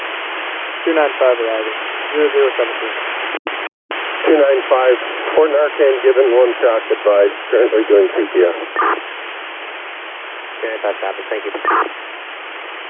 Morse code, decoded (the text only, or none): none